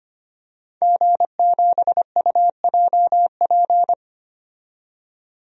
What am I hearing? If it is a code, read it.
Morse code, heard as a keyed tone, about 25 wpm, 700 Hz. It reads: G7UJP
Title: DXCC-100 Call Signs - 5 Characters - 25wpm